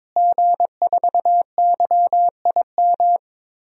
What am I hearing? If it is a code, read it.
Morse code, heard as a keyed tone, about 22 wpm, 700 Hz. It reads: G4YIM
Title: DXCC-100 Call Signs - 5 Characters - Repeated 3x - 22wpm